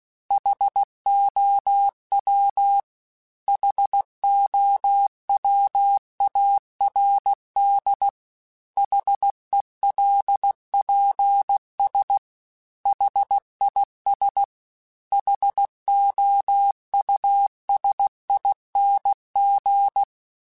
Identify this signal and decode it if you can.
Morse code, heard as a keyed tone, about 16 wpm, 800 Hz. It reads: HOW HOWARD HELPS HIS HOUSING